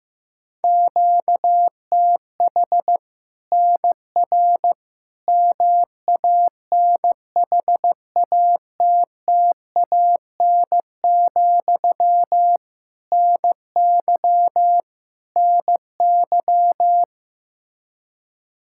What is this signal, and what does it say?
Morse code, heard as a keyed tone, about 15 wpm, 700 Hz. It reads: QTH NR MANHATTAN, NY NY
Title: Practice QSO - Exchange 1 - Beginning - 15wpm